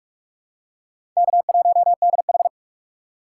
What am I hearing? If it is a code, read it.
Morse code, heard as a keyed tone, about 46 wpm, 700 Hz. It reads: K1DH
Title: USA Call Signs - 4 Characters - 6x Character Spacing - 45wpm